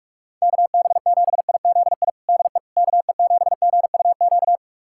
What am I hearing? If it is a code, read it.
Morse code, heard as a keyed tone, about 46 wpm, 700 Hz. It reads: KB7IZI DE KE7GUQ
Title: USA Callsign DE Callsign - 6 Characters - 45wpm